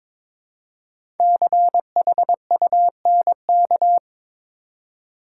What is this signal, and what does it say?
Morse code, heard as a keyed tone, about 22 wpm, 700 Hz. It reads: CHUNK
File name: 5 Letter Words - First 3 Letters Given - 22wpm